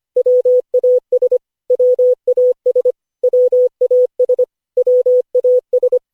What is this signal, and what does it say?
Morse code, heard as a keyed tone, about 25 wpm, 500 Hz. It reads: WAS WAS WAS WAS
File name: cw / Session5 cw training